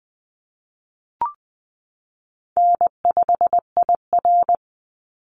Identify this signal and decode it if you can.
Morse code, heard as a keyed tone, about 20 wpm, 700 Hz. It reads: N5IR
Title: Easy Call Signs Repeated 2x - 20wpm